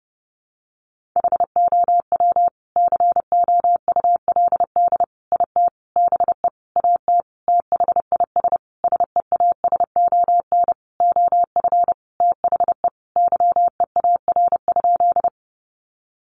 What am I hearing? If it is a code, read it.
Morse code, heard as a keyed tone, about 30 wpm, 700 Hz. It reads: HOW COULD IT BE AT THIS SEASON OF THE YEAR?
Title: Sentences from Top 1400 Words 30wpm